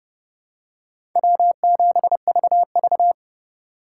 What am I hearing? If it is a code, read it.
Morse code, heard as a keyed tone, about 29 wpm, 700 Hz. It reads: W7VV